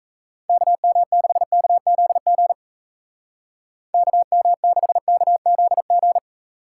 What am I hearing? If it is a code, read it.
Morse code, heard as a keyed tone, about 38 wpm, 700 Hz. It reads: KM6KZG KM6KZG